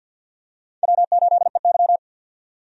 Morse code, heard as a keyed tone, about 50 wpm, 700 Hz. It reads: W8EY
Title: USA Call Signs - 4 Characters - 50wpm